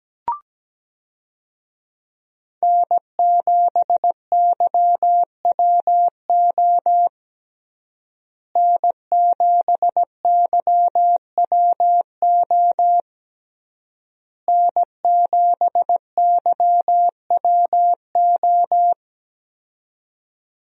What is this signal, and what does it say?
Morse code, heard as a keyed tone, about 17 wpm, 700 Hz. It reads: N7YWO N7YWO N7YWO
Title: USA Call Signs - 5 Characters - Repeated 3x - 17wpm